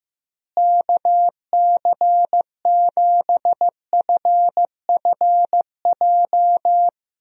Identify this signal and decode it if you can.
Morse code, heard as a keyed tone, about 15 wpm, 700 Hz. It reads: KC7FFJ
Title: USA Call Signs - 6 Characters - Repeated 2x - 15wpm